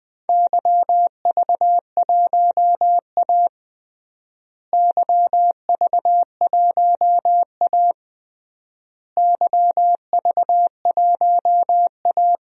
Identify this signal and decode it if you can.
Morse code, heard as a keyed tone, about 20 wpm, 700 Hz. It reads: YV1A YV1A YV1A
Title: DXCC-100 Call Signs - 4 Characters - Repeated 3x - 20wpm